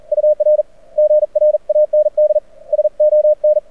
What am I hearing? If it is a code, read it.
Morse code, heard as a keyed tone, about 42 wpm, spaced as for 35 wpm, 590 Hz. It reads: UR GRANDSON
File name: archived / CRSnet